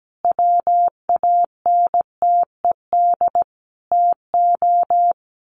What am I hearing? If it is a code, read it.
Morse code, heard as a keyed tone, about 17 wpm, 700 Hz. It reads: WANTED TO
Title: A Sound Of Thunder 17wpm